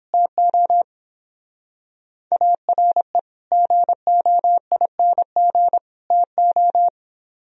Morse code, heard as a keyed tone, about 31 wpm, 700 Hz. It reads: TO ARE GOING TO